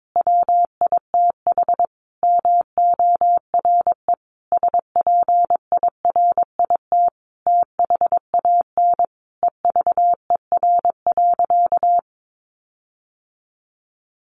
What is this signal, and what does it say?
Morse code, heard as a keyed tone, about 22 wpm, 700 Hz. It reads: WITH MORE SPIRIT THAN EVER.
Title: Sentences from Top 1400 Words 22wpm